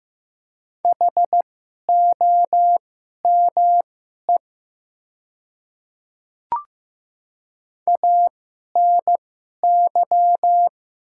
Morse code, heard as a keyed tone, about 15 wpm, spaced as for 11 wpm, 700 Hz. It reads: HOME ANY
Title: 2x Character Spacing - Top 200 Words - 15wpm